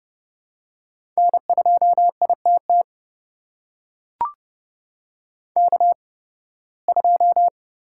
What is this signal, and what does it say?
Morse code, heard as a keyed tone, about 30 wpm, spaced as for 19 wpm, 700 Hz. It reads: N2ITT K2
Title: USA Call Signs - 8x Character Spacing - 30wpm